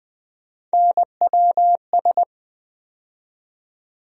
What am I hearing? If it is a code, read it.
Morse code, heard as a keyed tone, about 21 wpm, 700 Hz. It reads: NWS
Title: USA Call Sign Postfixes - 3 Letters 20wpm